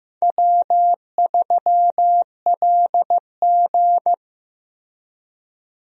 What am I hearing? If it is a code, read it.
Morse code, heard as a keyed tone, about 15 wpm, 700 Hz. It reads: W3LG
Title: USA Call Signs - 4 Characters - Repeated 3x - 15wpm